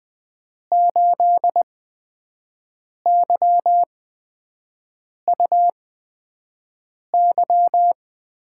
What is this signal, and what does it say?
Morse code, heard as a keyed tone, about 20 wpm, spaced as for 5 wpm, 700 Hz. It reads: 8YUY
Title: USA Call Signs - 8x Character Spacing - 20wpm